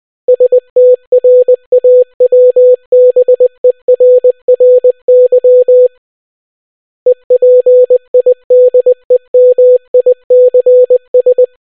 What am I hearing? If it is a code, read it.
Morse code, heard as a keyed tone, about 20 wpm, 500 Hz. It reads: STRAWBERRY EPIDEMICS